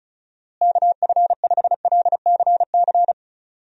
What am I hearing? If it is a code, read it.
Morse code, heard as a keyed tone, about 34 wpm, 700 Hz. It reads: KF5LCC
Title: USA Call Signs - 6 Characters - Repeated 2x - 35wpm